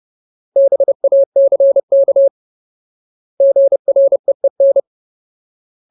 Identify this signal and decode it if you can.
Morse code, heard as a keyed tone, about 30 wpm, 550 Hz. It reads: BACK GREEN
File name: Ultimate Word List / 30WPM Ultimate